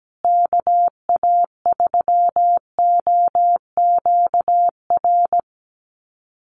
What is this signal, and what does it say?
Morse code, heard as a keyed tone, about 17 wpm, 700 Hz. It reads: KA3OQR
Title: USA Call Signs Repeated 3x 17wpm